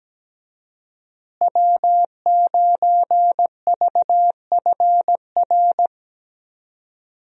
Morse code, heard as a keyed tone, about 17 wpm, 700 Hz. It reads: W9VFR
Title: USA Call Signs - 5 Characters - Repeated 3x - 17wpm